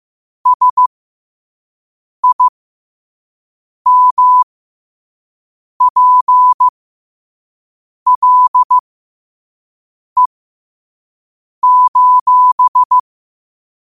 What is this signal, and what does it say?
Morse code, heard as a keyed tone, about 15 wpm, spaced as for 5 wpm, 1000 Hz. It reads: SIMPLE:
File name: qotdfiles / 5wpm